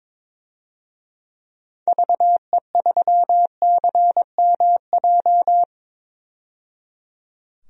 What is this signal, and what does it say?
Morse code, heard as a keyed tone, about 22 wpm, 700 Hz. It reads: VE3CMJ
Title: DXCC-100 Call Signs - 6 Characters - 22wpm